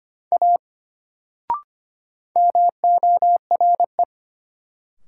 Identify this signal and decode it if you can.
Morse code, heard as a keyed tone, about 25 wpm, 700 Hz. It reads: A MORE
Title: Sentences from Top 100 Words - New Word Drill - 25wpm